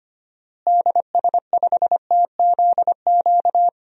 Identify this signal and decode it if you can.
Morse code, heard as a keyed tone, about 25 wpm, 700 Hz. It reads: DS5TZQ
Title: DXCC-100 Call Signs - 6 Characters - Repeated 3x - 25wpm